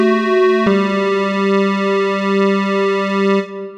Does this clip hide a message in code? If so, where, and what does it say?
No.